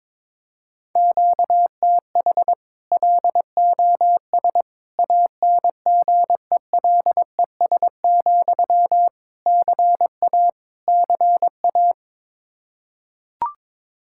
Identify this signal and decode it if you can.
Morse code, heard as a keyed tone, about 21 wpm, 700 Hz. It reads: QTH LOS ANGELES, CA CA
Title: Practice QSO - Exchange 1 - Beginning - 22wpm